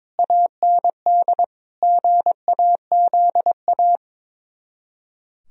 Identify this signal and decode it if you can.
Morse code, heard as a keyed tone, about 22 wpm, 700 Hz. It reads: AND GAZA
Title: Speed-Racing - Sets of 5 Words 22wpm